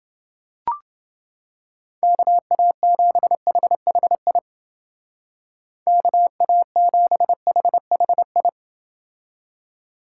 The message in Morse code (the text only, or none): KA7HHI KA7HHI